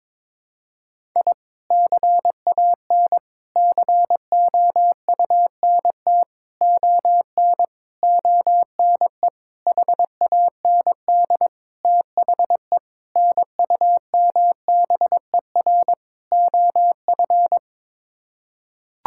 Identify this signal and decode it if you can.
Morse code, heard as a keyed tone, about 21 wpm, 700 Hz. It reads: I CAN COUNT ON ONE HAND THE NUMBER OF
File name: Sets of 9 Words - 22wpm